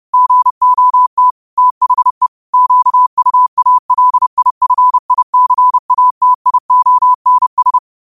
Morse code, heard as a keyed tone, about 30 wpm, 1000 Hz. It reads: GOT THE QUALIFICATIONS
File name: qotdfiles / 30wpm